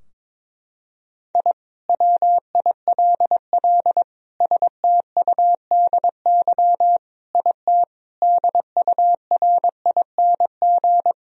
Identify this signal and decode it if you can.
Morse code, heard as a keyed tone, about 22 wpm, 700 Hz. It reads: I WILL STUDY IT DURING
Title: Sentences from Top 1000 Words - Review Part 2 - 22wpm